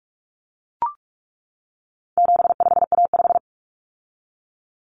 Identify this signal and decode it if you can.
Morse code, heard as a keyed tone, about 44 wpm, 700 Hz. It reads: 75A5
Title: Common Cut Numbers - 4-Digit 45wpm